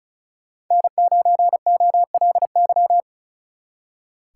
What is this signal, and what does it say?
Morse code, heard as a keyed tone, about 43 wpm, 700 Hz. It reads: N9OLY
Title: Speed Racing - USA Call Signs 35wpm